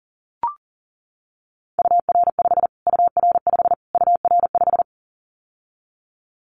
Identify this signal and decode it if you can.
Morse code, heard as a keyed tone, about 37 wpm, 700 Hz. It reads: UR5 UR5 UR5
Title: Callsign Prefixes for Top 25 DXCC - Repeated 3x - 40wpm